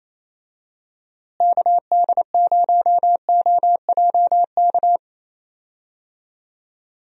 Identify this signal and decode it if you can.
Morse code, heard as a keyed tone, about 28 wpm, 700 Hz. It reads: KD0OJK
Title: USA Call Signs - 6 Characters - Repeated 3x - 28wpm